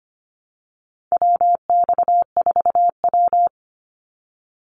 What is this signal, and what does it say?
Morse code, heard as a keyed tone, about 25 wpm, 700 Hz. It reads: WX4W
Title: Call Signs Repeated 3x - 25wpm